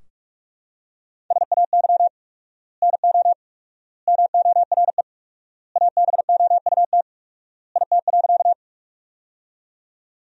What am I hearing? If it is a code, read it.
Morse code, heard as a keyed tone, about 41 wpm, 700 Hz. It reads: SAY NO MORE ABOUT IT.
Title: 4x Word-Spacing - Sentences from Top 200 Words - 45wpm